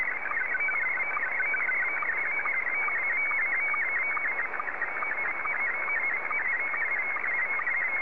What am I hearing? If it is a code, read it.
SSTV (slow-scan television), Scottie 1.
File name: sstv / S1-JA3QJQ